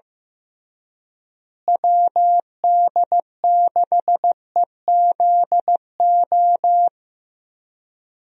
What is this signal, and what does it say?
Morse code, heard as a keyed tone, about 15 wpm, 700 Hz. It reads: WD6EZO
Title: USA Call Signs - 6 Characters - Repeated 3x - 15wpm